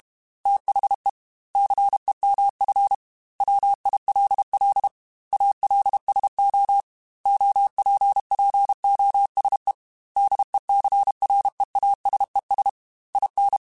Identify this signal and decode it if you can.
Morse code, heard as a keyed tone, about 32 wpm, 800 Hz. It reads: THE CEMF WILL ALSO OPPOSE DECREASES IN
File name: Technical Subjects / Technical Subjects-mp3s / XL @30